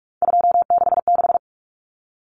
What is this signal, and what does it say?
Morse code, heard as a keyed tone, about 44 wpm, 700 Hz. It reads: 266
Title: Common Cut Numbers - 3-Digit 45wpm